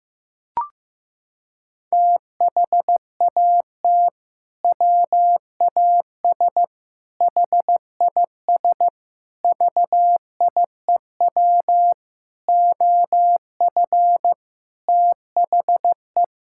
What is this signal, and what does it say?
Morse code, heard as a keyed tone, about 15 wpm, 700 Hz. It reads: THAT WAS HIS VIEW OF THE
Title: Sentences from Top 1000 Words - Review Part 6 - 15wpm